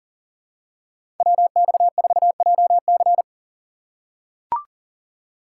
Morse code, heard as a keyed tone, about 38 wpm, 700 Hz. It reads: WX4JC
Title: USA Call Signs - 5 Characters - 4x Character Spacing - 40wpm